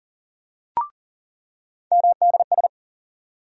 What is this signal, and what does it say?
Morse code, heard as a keyed tone, about 40 wpm, 700 Hz. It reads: MDS
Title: USA Call Sign Postfixes - 3 Letters 40wpm